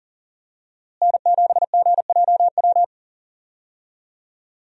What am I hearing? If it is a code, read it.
Morse code, heard as a keyed tone, about 40 wpm, 700 Hz. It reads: N7GJW
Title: USA Call Signs - 5 Characters - Repeated 2x - 40wpm